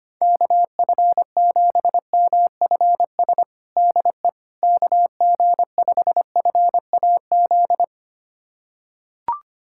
Morse code, heard as a keyed tone, about 25 wpm, 700 Hz. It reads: KF7MFS DE KG5FAZ
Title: USA Callsign DE Callsign - 6 Characters - 25wpm